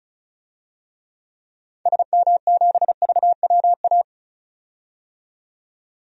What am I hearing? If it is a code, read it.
Morse code, heard as a keyed tone, about 34 wpm, 700 Hz. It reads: SM7VWA
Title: DXCC-100 Call Signs - 6 Characters - 35wpm